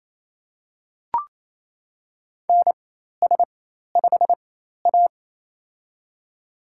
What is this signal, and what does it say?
Morse code, heard as a keyed tone, about 28 wpm, spaced as for 13 wpm, 700 Hz. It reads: NS5A
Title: USA Call Signs - 4 Characters - 4x Character Spacing - 28wpm